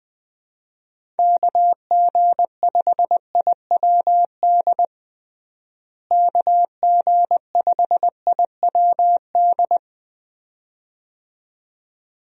Morse code, heard as a keyed tone, about 20 wpm, 700 Hz. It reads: KG5IWD KG5IWD
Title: USA Call Signs - 6 Characters - Repeated 2x - 20wpm